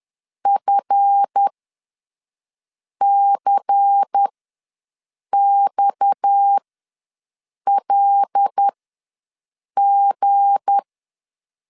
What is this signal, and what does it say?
Morse code, heard as a keyed tone, about 10 wpm, spaced as for 5 wpm, 800 Hz. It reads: FCXLG
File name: LEZIONI / 7 GF9Z / 7 GF ws 10wpm 3918